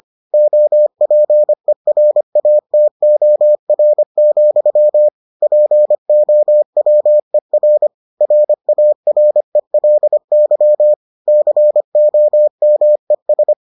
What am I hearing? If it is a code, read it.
Morse code, heard as a keyed tone, about 25 wpm, 600 Hz. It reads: OPERATOR, POWER RARELY COMES